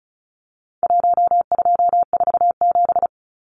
Morse code, heard as a keyed tone, about 34 wpm, 700 Hz. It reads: 1247
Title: Numbers - Contest Style - 4-Digit 35wpm